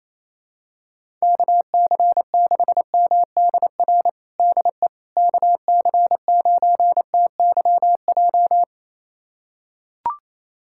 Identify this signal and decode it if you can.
Morse code, heard as a keyed tone, about 27 wpm, 700 Hz. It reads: KC6MDR DE KC9TYJ
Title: USA Callsign DE Callsign - 6 Characters - 28wpm